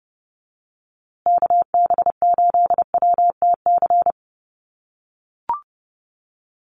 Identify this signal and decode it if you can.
Morse code, heard as a keyed tone, about 30 wpm, 700 Hz. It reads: KB8WTC
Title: USA Call Signs Repeated 2x 30wpm